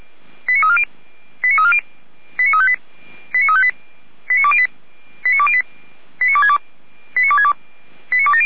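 Five-tone selective call (ZVEI), heard as quick runs of five tones, tones of about 70 ms. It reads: 89340 89358 89358 89208 89208 89262 89262 89270
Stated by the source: ZVEI five-tone selective call 89230 received on SDR.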